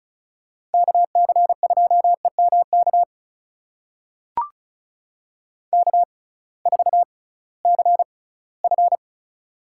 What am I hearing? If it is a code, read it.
Morse code, heard as a keyed tone, about 36 wpm, spaced as for 20 wpm, 700 Hz. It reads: KC2EMK K4CF
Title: USA Call Signs - 6x Character Spacing - 35wpm